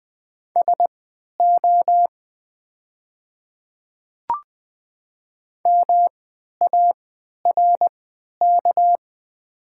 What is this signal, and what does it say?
Morse code, heard as a keyed tone, about 20 wpm, spaced as for 11 wpm, 690 Hz. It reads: SO MARK
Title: 3x Character Spacing - Top 300 Words - 20wpm